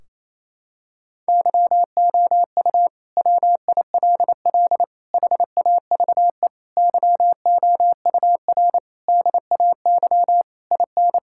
Morse code, heard as a keyed tone, about 28 wpm, 700 Hz. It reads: YOU WILL HAVE YOUR DAY IN
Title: Sentences from Top 1600 Words 28wpm